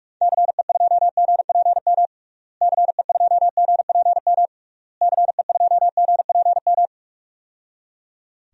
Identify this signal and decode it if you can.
Morse code, heard as a keyed tone, about 41 wpm, 700 Hz. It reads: CE2GPM CE2GPM CE2GPM
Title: DXCC-100 Call Signs - 6 Characters - Repeated 3x - 45wpm